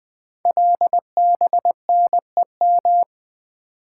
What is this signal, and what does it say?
Morse code, heard as a keyed tone, about 20 wpm, 700 Hz. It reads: LBNEM
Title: Code Groups - 5 Letters - 20wpm